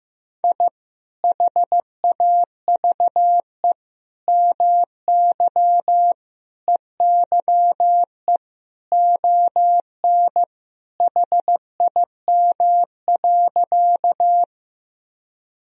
Morse code, heard as a keyed tone, about 16 wpm, 700 Hz. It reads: I HAVE MY EYE ON HIM.
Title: Sentences from Top 1000 Words - Review Part 1 - 15wpm